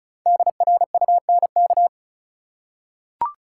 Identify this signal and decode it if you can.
Morse code, heard as a keyed tone, about 34 wpm, 700 Hz. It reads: DRUNK
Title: Five Letter Words - 35wpm